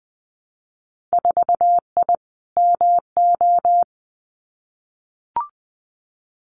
Morse code, heard as a keyed tone, about 20 wpm, 700 Hz. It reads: 4I MO
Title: Winter Field Day Exchanges 20wpm